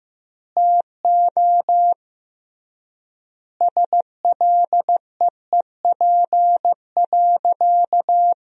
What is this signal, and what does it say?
Morse code, heard as a keyed tone, about 15 wpm, 700 Hz. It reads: TO SLEEP.
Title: 3x Word-Spacing - Sentences from Top 500 Words - 15wpm